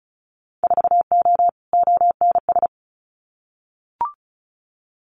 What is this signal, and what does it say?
Morse code, heard as a keyed tone, about 34 wpm, 700 Hz. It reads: 4O ONS
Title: Winter Field Day Exchanges 35wpm